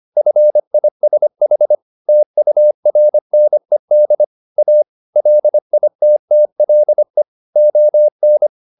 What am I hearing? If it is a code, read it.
Morse code, heard as a keyed tone, about 25 wpm, 600 Hz. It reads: FISH TURNED A LITTLE ON